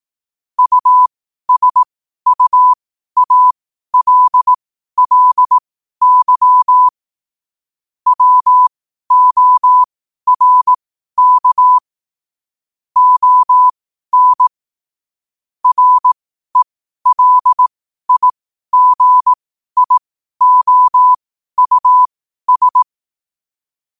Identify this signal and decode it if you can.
Morse code, heard as a keyed tone, about 18 wpm, spaced as for 12 wpm, 1000 Hz. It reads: USUALLY WORK ON RELIGIOUS
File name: qotdfiles / 12wpm